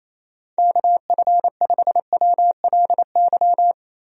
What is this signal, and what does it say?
Morse code, heard as a keyed tone, about 28 wpm, 700 Hz. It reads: KF5WLY
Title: USA Call Signs - 6 Characters - 28wpm